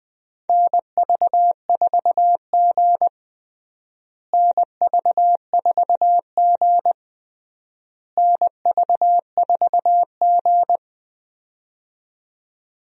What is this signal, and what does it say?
Morse code, heard as a keyed tone, about 21 wpm, 700 Hz. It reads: NV4G NV4G NV4G
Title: USA Call Signs - 4 Characters - Repeated 3x - 20wpm